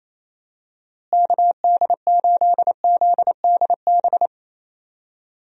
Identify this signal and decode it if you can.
Morse code, heard as a keyed tone, about 28 wpm, 700 Hz. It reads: KD8ZDB
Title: USA Call Signs - 6 Characters - Repeated 3x - 28wpm